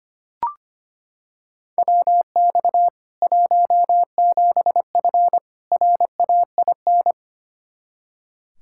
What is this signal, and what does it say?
Morse code, heard as a keyed tone, about 25 wpm, 700 Hz. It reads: WX 17F RAIN
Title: K1USN SST QSO Practice 25wpm